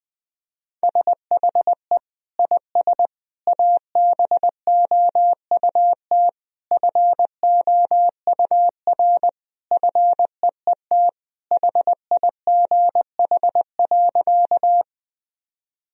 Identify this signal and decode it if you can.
Morse code, heard as a keyed tone, about 19 wpm, 700 Hz. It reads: SHE IS ABOUT FOUR FEET HIGH.
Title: Sentences from Top 1000 Words - Review Part 1 - 20wpm